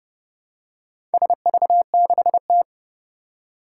Morse code, heard as a keyed tone, about 30 wpm, 700 Hz. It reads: SV6T
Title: Four Challenging Letter-Number Combinations 30wpm